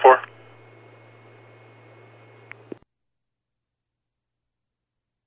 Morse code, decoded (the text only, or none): none